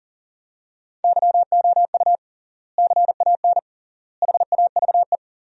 Morse code, heard as a keyed tone, about 40 wpm, 700 Hz. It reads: YOU CAN HAVE
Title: 3x Word-Spacing - Sentences from Top 100 Words - 40wpm